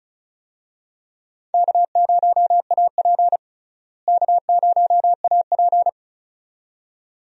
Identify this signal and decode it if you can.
Morse code, heard as a keyed tone, about 36 wpm, 700 Hz. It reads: K0AP K0AP
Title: USA Call Signs - 4 Characters - Repeated 2x - 35wpm